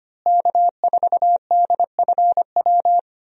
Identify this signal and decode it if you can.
Morse code, heard as a keyed tone, about 25 wpm, 700 Hz. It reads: K4DFW
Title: USA Call Signs - 5 Characters - Repeated 2x - 25wpm